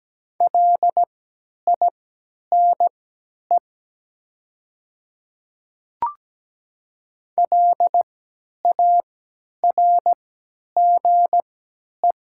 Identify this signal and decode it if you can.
Morse code, heard as a keyed tone, about 17 wpm, spaced as for 10 wpm, 700 Hz. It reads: LINE LARGE
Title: 3x Character Spacing - Top 300 Words - 17wpm